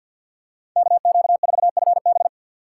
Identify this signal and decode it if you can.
Morse code, heard as a keyed tone, about 47 wpm, 700 Hz. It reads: KQ4FB
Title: USA Call Signs - 5 Characters - 3x Character Spacing - 50wpm